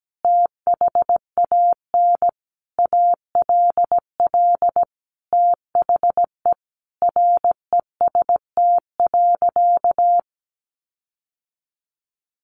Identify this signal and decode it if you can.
Morse code, heard as a keyed tone, about 17 wpm, 700 Hz. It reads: THAN ALL THE REST.
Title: Sentences from Top 1300 Words 17wpm